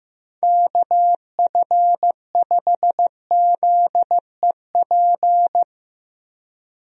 Morse code, heard as a keyed tone, about 15 wpm, 700 Hz. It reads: KF5ZEP
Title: USA Call Signs - 6 Characters - 15wpm